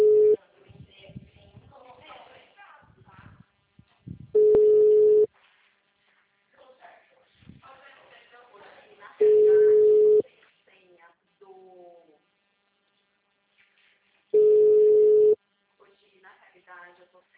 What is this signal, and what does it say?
Ringback tone (Europe, 425 Hz), with speech.